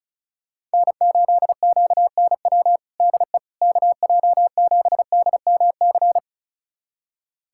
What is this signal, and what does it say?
Morse code, heard as a keyed tone, about 36 wpm, 700 Hz. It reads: N8QNW DE KJ7DMC
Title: USA Callsign DE Callsign 35wpm